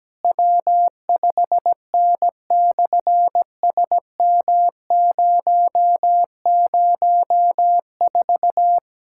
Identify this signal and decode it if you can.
Morse code, heard as a keyed tone, about 17 wpm, 700 Hz. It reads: W5N/SM004
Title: SOTA QSO Practice 17wpm